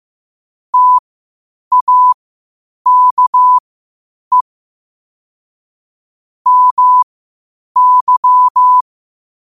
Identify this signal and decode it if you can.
Morse code, heard as a keyed tone, about 15 wpm, spaced as for 8 wpm, 1000 Hz. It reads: TAKE MY
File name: qotdfiles / 8wpm